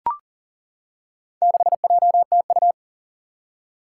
Morse code, heard as a keyed tone, about 40 wpm, 700 Hz. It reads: 6JTU
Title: Four Challenging Letter-Number Combinations 40wpm